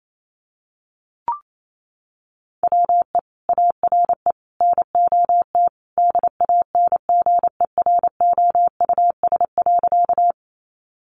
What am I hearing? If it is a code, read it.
Morse code, heard as a keyed tone, about 28 wpm, 700 Hz. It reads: WE ARE NOT DANGEROUS.
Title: Sentences from Top 1200 Words 28wpm